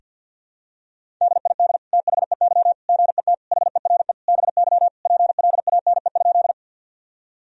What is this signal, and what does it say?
Morse code, heard as a keyed tone, about 50 wpm, 700 Hz. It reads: DID THEY GET HERE BY PLANE?
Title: Sentences from Top 1000 Words - Review Part 3 - 50wpm